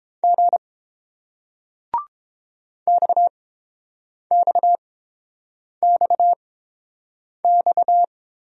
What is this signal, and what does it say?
Morse code, heard as a keyed tone, about 29 wpm, spaced as for 7 wpm, 700 Hz. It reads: G XXXX